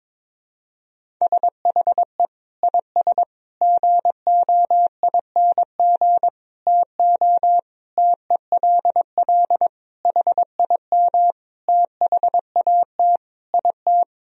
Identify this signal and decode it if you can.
Morse code, heard as a keyed tone, about 22 wpm, 700 Hz. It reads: SHE IS GOING TO TELL HIM THAT IT
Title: Sentences from Top 1000 Words - Review Part 1 - 22wpm